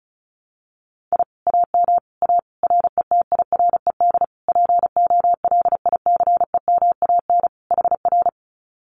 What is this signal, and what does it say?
Morse code, heard as a keyed tone, about 34 wpm, 700 Hz. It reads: I AM A RETIRED POLICEMAN HR
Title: Common QSO Phrases Encore 35wpm